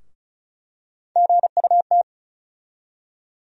Morse code, heard as a keyed tone, about 36 wpm, 700 Hz. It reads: GUT
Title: USA Call Sign Postfixes - 3 Letters 35wpm